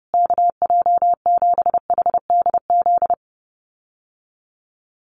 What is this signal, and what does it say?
Morse code, heard as a keyed tone, about 30 wpm, 700 Hz. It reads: KJ7HDZ
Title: USA Call Signs Repeated 3x 30wpm